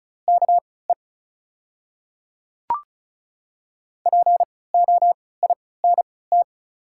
Morse code, heard as a keyed tone, about 36 wpm, spaced as for 20 wpm, 700 Hz. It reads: KE POINT
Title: 3x Character Spacing - Top 200 Words - 35wpm